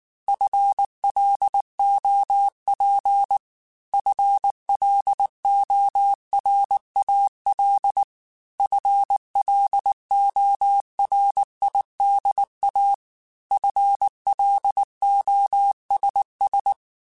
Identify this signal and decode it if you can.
Morse code, heard as a keyed tone, about 19 wpm, 800 Hz. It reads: FLOP FLORAL FLORIDA FLOSS
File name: FL Runs / FL Runs mp3s / FL Runs @18